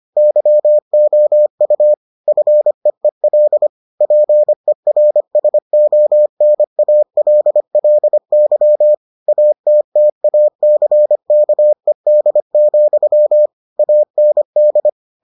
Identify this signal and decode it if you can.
Morse code, heard as a keyed tone, about 25 wpm, 600 Hz. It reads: YOU FEEL PERSONALLY ATTACKED, AND